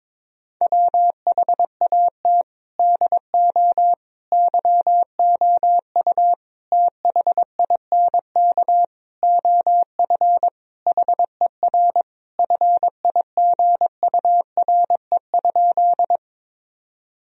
Morse code, heard as a keyed tone, about 22 wpm, 700 Hz. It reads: WHAT DO YOU THINK OF HER FIGURE?
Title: Sentences from Top 1000 Words - Review Part 2 - 22wpm